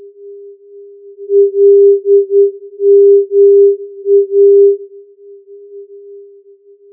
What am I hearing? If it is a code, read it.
Morse code, heard as a keyed tone, about 9 wpm, 400 Hz. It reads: LMA
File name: filtered morse